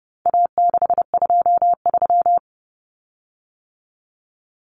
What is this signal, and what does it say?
Morse code, heard as a keyed tone, about 30 wpm, 700 Hz. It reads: A623
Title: Common Cut Numbers - 4-Digit 30wpm